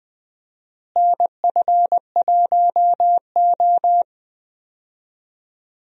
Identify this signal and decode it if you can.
Morse code, heard as a keyed tone, about 20 wpm, 700 Hz. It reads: NF1O
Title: USA Call Signs - 4 Characters - Repeated 2x - 20wpm